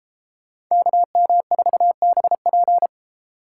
Morse code, heard as a keyed tone, about 34 wpm, 700 Hz. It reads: KM4BP
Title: Speed Racing - USA 2x2 Call Signs 28wpm